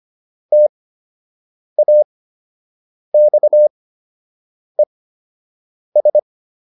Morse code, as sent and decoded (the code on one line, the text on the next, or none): - .- -..- . ...
TAXES